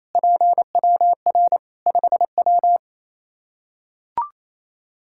Morse code, heard as a keyed tone, about 28 wpm, 700 Hz. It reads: PWR 5W
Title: Practice QSO - Exchange 2 - Info - 28wpm